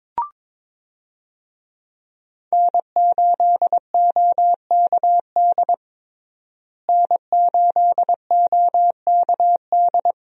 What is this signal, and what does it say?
Morse code, heard as a keyed tone, about 22 wpm, 700 Hz. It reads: N8OKD N8OKD
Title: USA Call Signs - 5 Characters - Repeated 2x - 22wpm